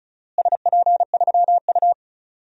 Morse code, heard as a keyed tone, about 36 wpm, 700 Hz. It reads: SP3U
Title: DXCC-100 Call Signs - 4 Characters - Repeated 2x - 35wpm